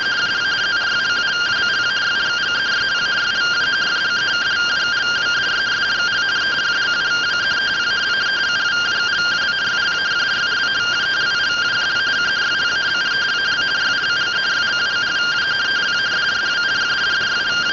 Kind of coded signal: two-tone teleprinter signal

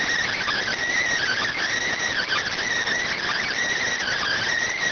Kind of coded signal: Emergency Alert System (SAME) tones

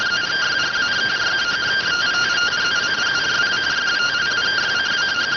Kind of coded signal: two-tone teleprinter signal

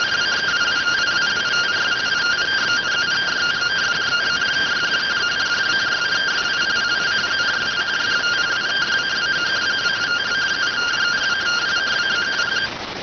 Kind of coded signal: two-tone teleprinter signal

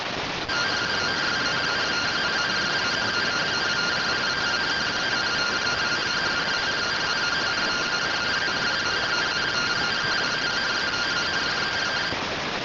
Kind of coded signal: two-tone teleprinter signal

